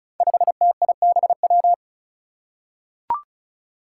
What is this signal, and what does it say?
Morse code, heard as a keyed tone, about 34 wpm, 700 Hz. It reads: 5TIBW
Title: Five Challenging Letter-Number Combinations 35wpm